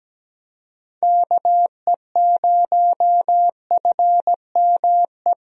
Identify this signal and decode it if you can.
Morse code, heard as a keyed tone, about 17 wpm, 700 Hz. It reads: KE0FME
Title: USA Call Signs - 6 Characters - Repeated 3x - 17wpm